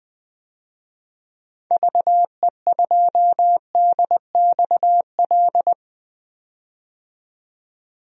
Morse code, heard as a keyed tone, about 20 wpm, 700 Hz. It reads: VE2DXL
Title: DXCC-100 Call Signs - 6 Characters - 20wpm